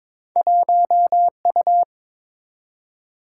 Morse code, heard as a keyed tone, about 22 wpm, 700 Hz. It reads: 1U
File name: practice / 22wpm / Two Letter-Number Combinations - 22wpm